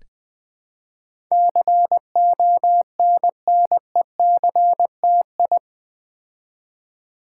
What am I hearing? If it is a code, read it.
Morse code, heard as a keyed tone, about 20 wpm, 700 Hz. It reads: CONNECTI
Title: Build A Word - 4 Letters At A Time - 10-Letter Words - 20wpm